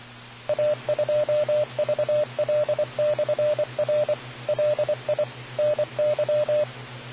Morse code, heard as a keyed tone, about 24 wpm, 600 Hz, with radio noise. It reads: A2VL/R LI NY